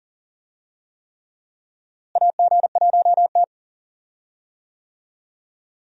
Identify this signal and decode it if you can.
Morse code, heard as a keyed tone, about 40 wpm, 700 Hz. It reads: AG1T
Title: USA Call Signs - 4 Characters - Repeated 3x - 40wpm